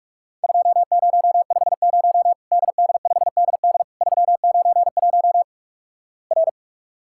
Morse code, heard as a keyed tone, about 46 wpm, 700 Hz. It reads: 2050 DD5DD 391 R
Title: Worked All Europe DX Contest - QTC Exchange - 45wpm